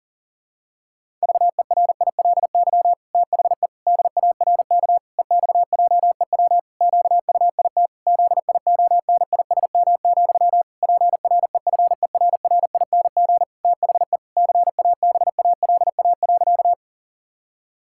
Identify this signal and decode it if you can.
Morse code, heard as a keyed tone, about 37 wpm, 700 Hz. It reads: VERILY THE DARK EXJEW QUIT ZIONISM, PREFERRING THE CABALA.
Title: Pangram Challenge v2 - 40wpm